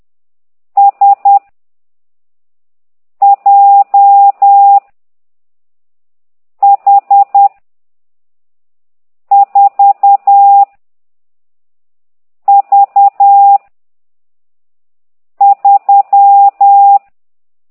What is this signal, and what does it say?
Morse code, heard as a keyed tone, about 10 wpm, 800 Hz. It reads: S J H 4 V 3